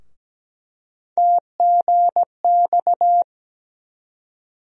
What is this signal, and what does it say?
Morse code, heard as a keyed tone, about 17 wpm, 700 Hz. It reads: TGX
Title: USA Call Sign Postfixes - 3 Letters 17wpm